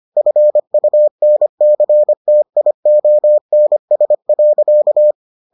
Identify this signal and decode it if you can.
Morse code, heard as a keyed tone, about 25 wpm, 600 Hz. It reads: FUNCTIONS.